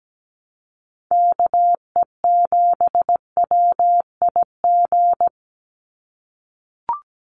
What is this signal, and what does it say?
Morse code, heard as a keyed tone, about 17 wpm, 700 Hz. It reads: KE7WIG
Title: USA Call Signs 17wpm